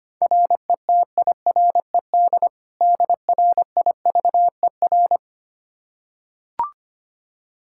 Morse code, heard as a keyed tone, about 25 wpm, 700 Hz. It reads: RETIRED DRIVER
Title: K1USN SST QSO Practice 25wpm